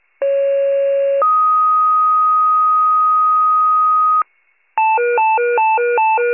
Two-tone page, heard as a one-second tone, then a three-second tone, about 550 then 1250 Hz. It is heard from 0.2 to 4.2 s.